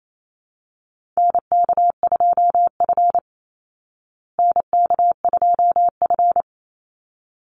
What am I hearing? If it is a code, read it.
Morse code, heard as a keyed tone, about 28 wpm, 700 Hz. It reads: NK2F NK2F
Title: Call Signs Repeated 3x - 28wpm